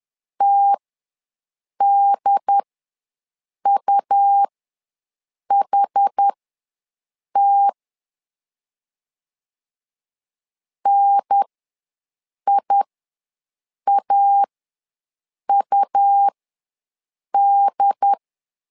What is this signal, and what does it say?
Morse code, heard as a keyed tone, about 10 wpm, spaced as for 6 wpm, 800 Hz. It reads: TDUHT NIAUD
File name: 2 UHDN ws 10wpm